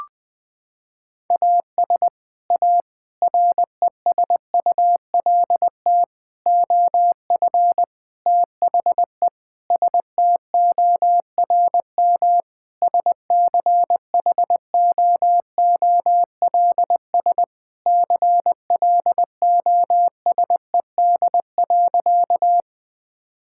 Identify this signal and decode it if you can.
Morse code, heard as a keyed tone, about 19 wpm, 700 Hz. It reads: AS A RESULT OF THE STORM SCHOOLS CLOSED.
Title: Sentences with 5-Word Phrases - 20wpm